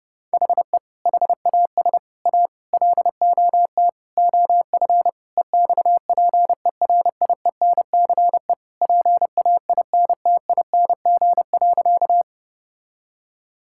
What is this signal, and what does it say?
Morse code, heard as a keyed tone, about 30 wpm, 700 Hz. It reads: HE HAS A LOT OF EXPERIENCE PAINTING.
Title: Sentences from Top 1000 Words - Review Part 6 - 30wpm